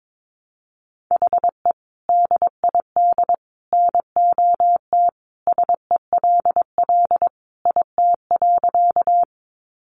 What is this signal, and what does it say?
Morse code, heard as a keyed tone, about 22 wpm, 700 Hz. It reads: HE DID NOT SELL IT.
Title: Sentences from Top 1100 Words 22wpm